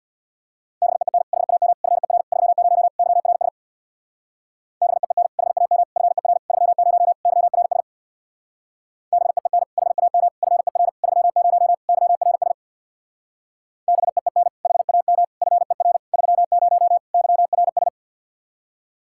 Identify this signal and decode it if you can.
Morse code, heard as a keyed tone, about 60 wpm, 700 Hz. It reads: BEEN HAM FER 30 YRS BEEN HAM FER 30 YRS BEEN HAM FER 30 YRS BEEN HAM FER 30 YRS
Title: Speed-Racing Common QSO Phrases Encore - 50wpm